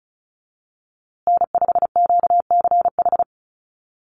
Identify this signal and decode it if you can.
Morse code, heard as a keyed tone, about 34 wpm, 700 Hz. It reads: N5QCH
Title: USA Call Signs Repeated 3x 35wpm